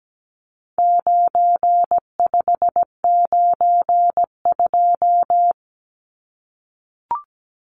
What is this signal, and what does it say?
Morse code, heard as a keyed tone, about 17 wpm, 700 Hz. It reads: 9592
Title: Numbers - 4-Digit 17wpm